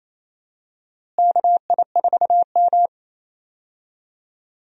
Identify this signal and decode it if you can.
Morse code, heard as a keyed tone, about 28 wpm, 700 Hz. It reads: KI4M
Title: USA Call Signs - 4 Characters - Repeated 2x - 28wpm